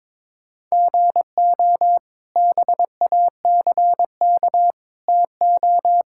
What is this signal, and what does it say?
Morse code, heard as a keyed tone, about 22 wpm, 700 Hz. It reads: GO BACK TO
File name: practice / 22wpm / Sentences with 2-Word Phrases - 22wpm